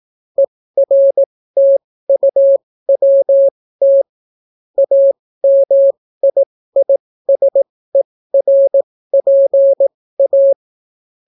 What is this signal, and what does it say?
Morse code, heard as a keyed tone, about 18 wpm, spaced as for 14 wpm, 550 Hz. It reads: ERTUWT AMIISERPA